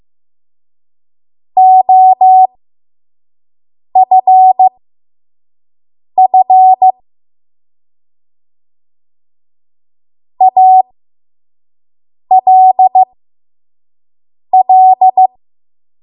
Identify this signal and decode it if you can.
Morse code, heard as a keyed tone, about 15 wpm, spaced as for 5 wpm, 750 Hz. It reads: OFF ALL
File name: Morse / 05WPM